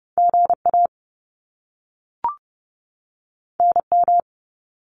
Morse code, heard as a keyed tone, about 30 wpm, 700 Hz. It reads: GA NM
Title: US States 30wpm